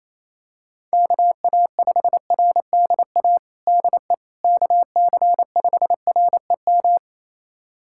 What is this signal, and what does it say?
Morse code, heard as a keyed tone, about 28 wpm, 700 Hz. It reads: KA5RDA DE KC5REM
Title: USA Callsign DE Callsign 28wpm